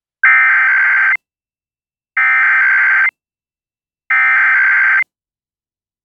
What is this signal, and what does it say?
Emergency Alert System (SAME) tones (header bursts), heard as harsh buzzing bursts. It is heard from 0.2 to 5.0 s.